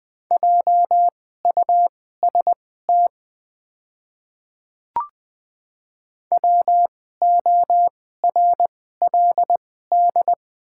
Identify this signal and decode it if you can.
Morse code, heard as a keyed tone, about 20 wpm, spaced as for 14 wpm, 700 Hz. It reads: JUST WORLD
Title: 2x Character Spacing - Top 200 Words - 20wpm